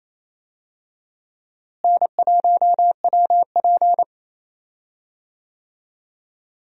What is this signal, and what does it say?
Morse code, heard as a keyed tone, about 28 wpm, 700 Hz. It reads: N1WP
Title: USA Call Signs - 4 Characters - Repeated 3x - 28wpm